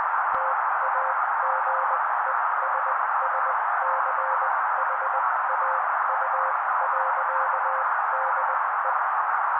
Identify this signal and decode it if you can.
Morse code, heard as a keyed tone, about 20 wpm, spaced as for 16 wpm, 530 Hz. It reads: TAGESSCHAU.DE
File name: daily / 16wpm noise